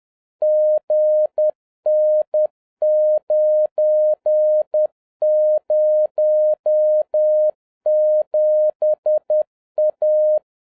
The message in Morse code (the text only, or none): GN907A